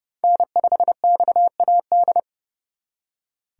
Morse code, heard as a keyed tone, about 34 wpm, 700 Hz. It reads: N5XAD
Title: Speed Racing - USA 1x3 Call Signs 30wpm